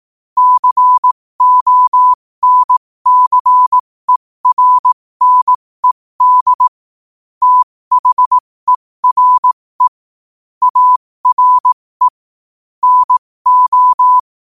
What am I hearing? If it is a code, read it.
Morse code, heard as a keyed tone, about 18 wpm, spaced as for 15 wpm, 1000 Hz. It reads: CONCERNED THERE ARE NO
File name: qotdfiles / 15wpm